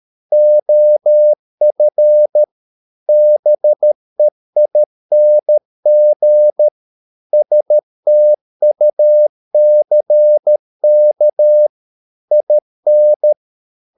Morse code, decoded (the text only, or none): OF BEING STUCK IN